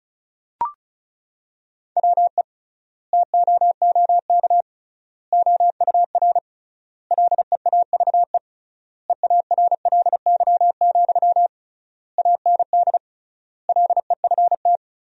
Morse code, heard as a keyed tone, about 34 wpm, 700 Hz. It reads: WE TOOK OUR LEAVE EARLY, AND LEFT
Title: 3x Word-Spacing - Sentences from Top 400 Words - 35wpm